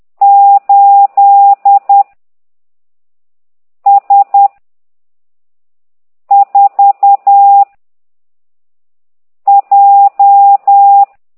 Morse code, heard as a keyed tone, about 10 wpm, 800 Hz. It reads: 8 S 4 J
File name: ct2idw / exerc15